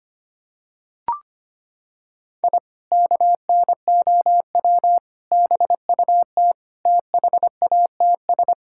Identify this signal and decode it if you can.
Morse code, heard as a keyed tone, about 25 wpm, 700 Hz. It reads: I KNOW BUT THATS
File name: practice / 25wpm / Sentences with 2-Word Phrases - 25wpm